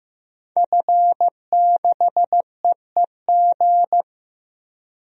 Morse code, heard as a keyed tone, about 15 wpm, 700 Hz. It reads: F6EEG
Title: DXCC-100 Call Signs - 5 Characters - 15wpm